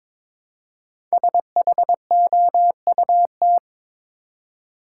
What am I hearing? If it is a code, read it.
Morse code, heard as a keyed tone, about 22 wpm, 700 Hz. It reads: SHOUT
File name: 5 Letter Words - First 3 Letters Given - 22wpm